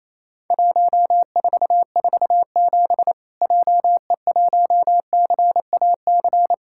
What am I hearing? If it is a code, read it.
Morse code, heard as a keyed tone, about 28 wpm, 700 Hz. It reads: 1447 JE1CAC
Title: Worked All Europe DX Contest - QTC Exchange - 28wpm